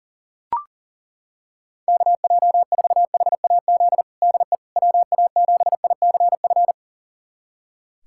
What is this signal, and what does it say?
Morse code, heard as a keyed tone, about 40 wpm, 700 Hz. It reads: KJ4HAZ DE WA7ICF
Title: USA Callsign DE Callsign - 6 Characters - 40wpm